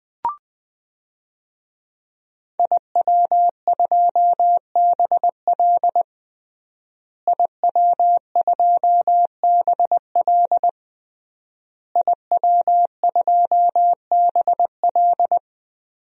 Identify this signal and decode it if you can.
Morse code, heard as a keyed tone, about 21 wpm, 700 Hz. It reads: IW2BL IW2BL IW2BL
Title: DXCC-100 Call Signs - 5 Characters - Repeated 3x - 20wpm